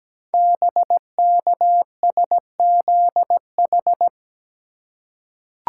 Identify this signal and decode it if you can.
Morse code, heard as a keyed tone, about 17 wpm, 700 Hz. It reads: BKSZH
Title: Code Groups - 5 Letters - 17wpm